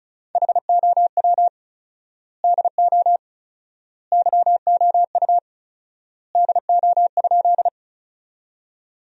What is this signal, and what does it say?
Morse code, heard as a keyed tone, about 36 wpm, 700 Hz. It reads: HOW DO YOU DO?
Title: 4x Word-Spacing - Sentences from Top 100 Words - 35wpm